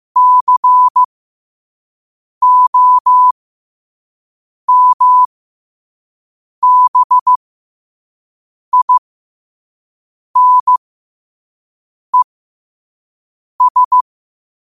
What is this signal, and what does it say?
Morse code, heard as a keyed tone, about 15 wpm, spaced as for 5 wpm, 1000 Hz. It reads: COMBINES